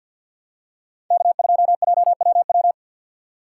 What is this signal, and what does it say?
Morse code, heard as a keyed tone, about 53 wpm, 700 Hz. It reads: K2JWW
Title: USA Call Signs - 5 Characters - 2x Character Spacing - 50wpm